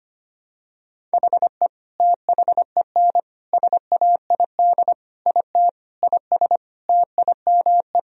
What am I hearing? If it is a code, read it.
Morse code, heard as a keyed tone, about 25 wpm, 700 Hz. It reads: HE THEN SAID IT IS TIME
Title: Sentences from Top 1000 Words - Review Part 1 - 25wpm